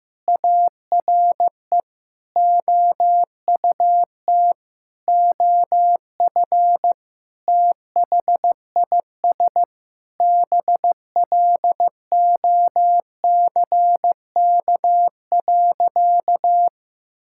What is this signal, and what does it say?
Morse code, heard as a keyed tone, about 15 wpm, 700 Hz. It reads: ARE OUT OF THIS BLOCK.
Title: Sentences from Top 1600 Words 15wpm